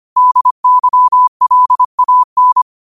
Morse code, heard as a keyed tone, about 25 wpm, 1000 Hz. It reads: DYLAN
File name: qotdfiles / 25wpm